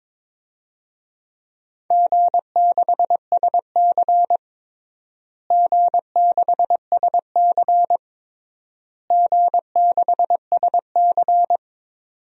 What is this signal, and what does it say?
Morse code, heard as a keyed tone, about 21 wpm, 700 Hz. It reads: G6SC G6SC G6SC
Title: DXCC-100 Call Signs - 4 Characters - Repeated 3x - 22wpm